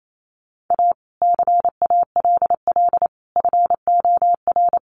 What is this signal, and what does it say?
Morse code, heard as a keyed tone, about 28 wpm, 700 Hz. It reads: A CALL FOR
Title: Sentences from Top 300 Words 28wpm